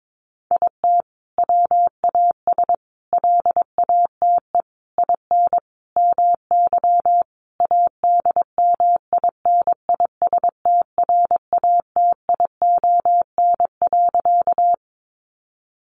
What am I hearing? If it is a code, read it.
Morse code, heard as a keyed tone, about 22 wpm, 700 Hz. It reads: IT WAS LATE IN MY ADMINISTRATION.
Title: Sentences from Top 1300 Words 22wpm